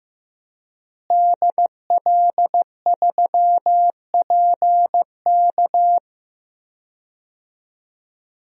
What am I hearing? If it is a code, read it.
Morse code, heard as a keyed tone, about 14 wpm, 700 Hz. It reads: DL3PK
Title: DXCC-100 Call Signs - 5 Characters - Repeated 2x - 15wpm